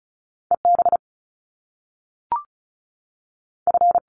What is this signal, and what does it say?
Morse code, heard as a keyed tone, about 36 wpm, 700 Hz. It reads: E B F
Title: Top Two Letter Combinations in Words 35wpm